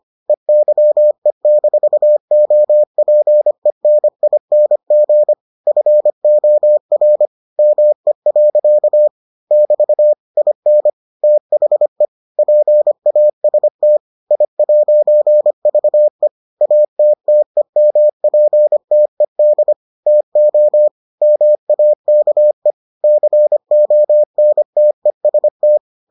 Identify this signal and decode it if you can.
Morse code, heard as a keyed tone, about 25 wpm, 600 Hz. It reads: EYE-OPENING FOR ME. <BT> IN THE PAST I'VE ATTEMPTED TO MAKE CONTEST